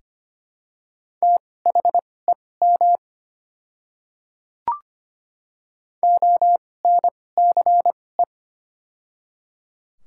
Morse code, heard as a keyed tone, about 25 wpm, spaced as for 18 wpm, 700 Hz. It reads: THEM ONCE
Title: 2x Character Spacing - Top 500 Words - 25wpm